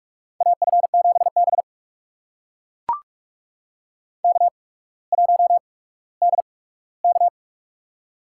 Morse code, heard as a keyed tone, about 44 wpm, spaced as for 19 wpm, 700 Hz. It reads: AF7B K1DK
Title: USA Call Signs - 4 Characters - 8x Character Spacing - 45wpm